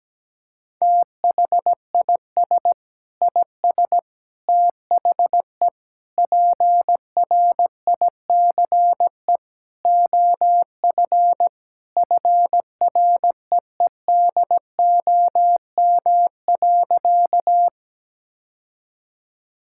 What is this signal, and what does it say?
Morse code, heard as a keyed tone, about 17 wpm, 700 Hz. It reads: THIS IS THE PRICE OF FREEDOM.
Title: Sentences from Top 1000 Words - Review Part 5 - 17wpm